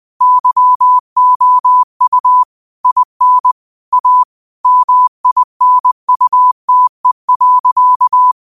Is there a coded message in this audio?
Yes: Morse code from 0.2 to 8.3 s, heard as a keyed tone, about 20 wpm, 1000 Hz. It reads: YOU IN A MINUTE.